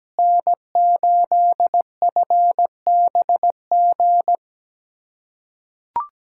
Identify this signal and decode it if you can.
Morse code, heard as a keyed tone, about 17 wpm, 700 Hz. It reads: N8FBG
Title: USA Callsign DE Callsign - 5 Characters - 17wpm